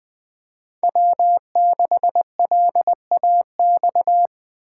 Morse code, heard as a keyed tone, about 20 wpm, 700 Hz. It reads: W6LAX
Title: K1USN SST QSO Practice - 20wpm